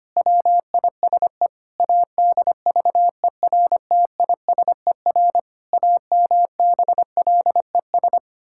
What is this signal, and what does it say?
Morse code, heard as a keyed tone, about 25 wpm, 700 Hz. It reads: WISE ADVERTISER AMBLES
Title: Pangram Challenge v2 - 25wpm